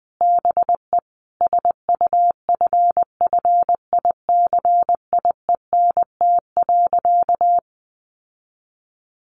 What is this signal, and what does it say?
Morse code, heard as a keyed tone, about 20 wpm, 700 Hz. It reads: BE SUFFICIENT.
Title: Sentences from Top 1400 Words 20wpm